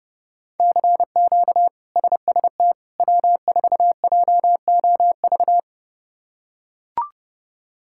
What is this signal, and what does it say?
Morse code, heard as a keyed tone, about 31 wpm, 700 Hz. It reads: CQ SST W4JOV
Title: K1USN SST QSO Practice - 30wpm